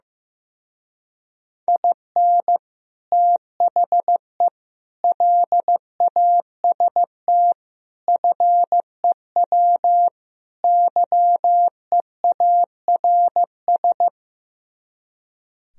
Morse code, heard as a keyed tone, about 15 wpm, 700 Hz. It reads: IN THE LAST FEW YEARS
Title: Speed-Racing - Sets of 5 Words 15wpm